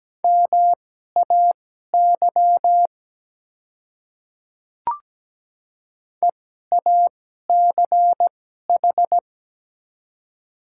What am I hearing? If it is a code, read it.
Morse code, heard as a keyed tone, about 17 wpm, spaced as for 12 wpm, 700 Hz. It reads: MAY EACH
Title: 2x Character Spacing - Top 100 Words - 17wpm